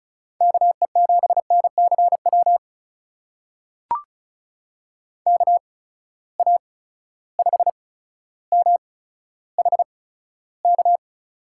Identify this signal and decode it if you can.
Morse code, heard as a keyed tone, about 36 wpm, spaced as for 15 wpm, 700 Hz. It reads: KE7NCW KA5MHK
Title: USA Call Signs - 6 Characters - 8x Character Spacing - 35wpm